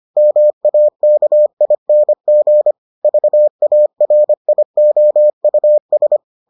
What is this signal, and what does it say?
Morse code, heard as a keyed tone, about 25 wpm, 600 Hz. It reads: MAKING VARIOUS